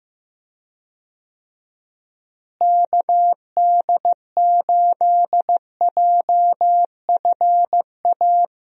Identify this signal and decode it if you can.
Morse code, heard as a keyed tone, about 15 wpm, 700 Hz. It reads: KD8JFA
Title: USA Call Signs - 6 Characters - Repeated 3x - 15wpm